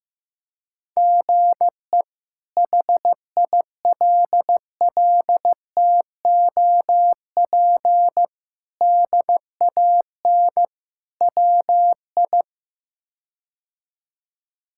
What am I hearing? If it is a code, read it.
Morse code, heard as a keyed tone, about 15 wpm, 700 Hz. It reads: GE HILLTOP DAN WI
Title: K1USN SST QSO Practice - 15wpm